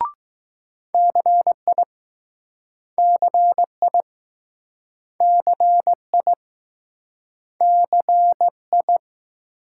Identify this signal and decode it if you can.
Morse code, heard as a keyed tone, about 20 wpm, 700 Hz. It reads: CI CI CI CI